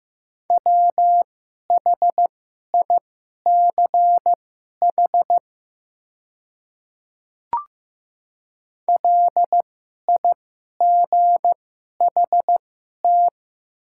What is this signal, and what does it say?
Morse code, heard as a keyed tone, about 15 wpm, spaced as for 11 wpm, 700 Hz. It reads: WHICH LIGHT
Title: 2x Character Spacing - Top 200 Words - 15wpm